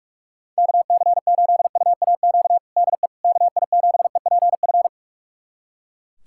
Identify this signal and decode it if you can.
Morse code, heard as a keyed tone, about 42 wpm, 700 Hz. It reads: KC8UAQ DE KI7EPF
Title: USA Callsign DE Callsign - 6 Characters - 45wpm